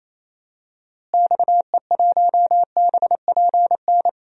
Morse code, heard as a keyed tone, about 28 wpm, 700 Hz. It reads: XE1BPN
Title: DXCC-100 Call Signs - 6 Characters - Repeated 2x - 28wpm